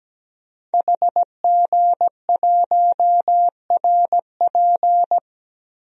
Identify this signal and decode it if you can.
Morse code, heard as a keyed tone, about 17 wpm, 700 Hz. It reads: HG1RP
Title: DXCC-100 Call Signs - 5 Characters - Repeated 3x - 17wpm